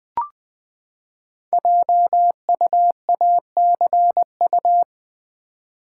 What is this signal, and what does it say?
Morse code, heard as a keyed tone, about 20 wpm, 700 Hz. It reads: JUACU
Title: Code Groups - 5 Letters - 20wpm